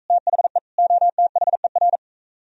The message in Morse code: THE OTHER

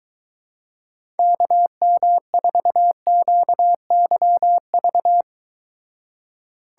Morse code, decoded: KM4QYV

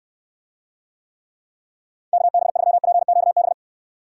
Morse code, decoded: KB3PQB

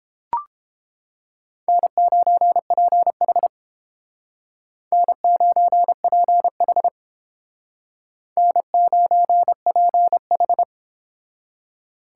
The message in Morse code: N9PH N9PH N9PH

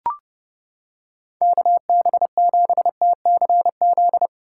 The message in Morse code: KB7TCZ

